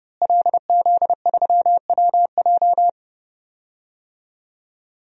Morse code, heard as a keyed tone, about 30 wpm, 700 Hz. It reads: LZ3WJ
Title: DXCC-100 Call Signs - 5 Characters - Repeated 2x - 30wpm